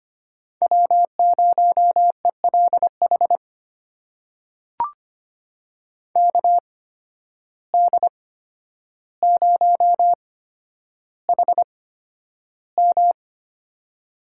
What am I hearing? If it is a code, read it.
Morse code, heard as a keyed tone, about 25 wpm, spaced as for 11 wpm, 700 Hz. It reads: W0ELH KD0HM
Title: USA Call Signs - 5 Characters - 8x Character Spacing - 25wpm